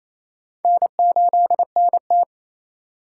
Morse code, heard as a keyed tone, about 37 wpm, 700 Hz. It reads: N8NT